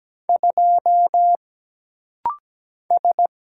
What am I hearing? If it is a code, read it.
Morse code, heard as a keyed tone, about 17 wpm, 700 Hz. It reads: 2 S
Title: Single Letter-Number Alt - Rapid-Fire 17wpm